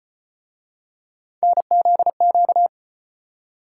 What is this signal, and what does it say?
Morse code, heard as a keyed tone, about 38 wpm, 700 Hz. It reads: NZQ